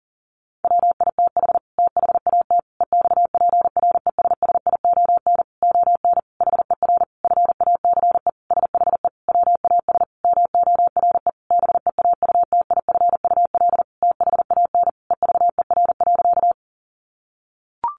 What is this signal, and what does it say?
Morse code, heard as a keyed tone, about 40 wpm, 700 Hz. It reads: WITH THAT EXPRESSION ON HER FACE SHE WAS MORE BEAUTIFUL THAN EVER.
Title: Sentences from Top 1200 Words 40wpm